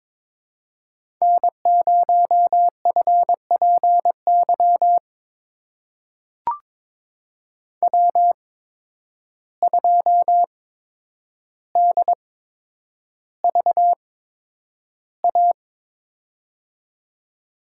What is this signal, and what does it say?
Morse code, heard as a keyed tone, about 21 wpm, spaced as for 9 wpm, 700 Hz. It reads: N0FPY W2DVA
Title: USA Call Signs - 5 Characters - 8x Character Spacing - 22wpm